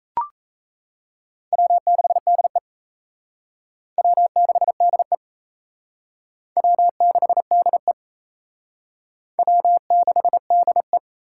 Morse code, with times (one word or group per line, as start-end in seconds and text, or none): 1.5-2.6 W6DE
4.0-5.2 W6DE
6.6-7.9 W6DE
9.4-11.0 W6DE